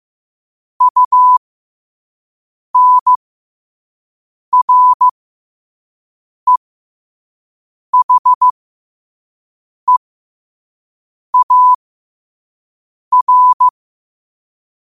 Morse code, heard as a keyed tone, about 15 wpm, spaced as for 5 wpm, 1000 Hz. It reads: UNREHEAR
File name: qotdfiles / 5wpm